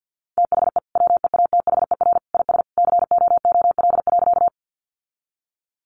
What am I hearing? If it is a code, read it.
Morse code, heard as a keyed tone, about 49 wpm, 700 Hz. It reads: THE WEATHER IS COOL.
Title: Sentences from Top 1200 Words 50wpm